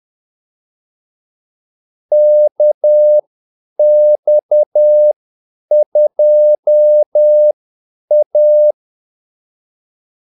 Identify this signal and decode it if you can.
Morse code, heard as a keyed tone, about 10 wpm, spaced as for 8 wpm, 600 Hz. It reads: KX2A
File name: Callsigns-10-8wpm